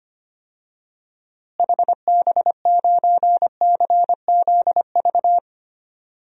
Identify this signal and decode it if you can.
Morse code, heard as a keyed tone, about 25 wpm, 700 Hz. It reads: HB9CZV
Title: DXCC-100 Call Signs - 6 Characters - Repeated 3x - 25wpm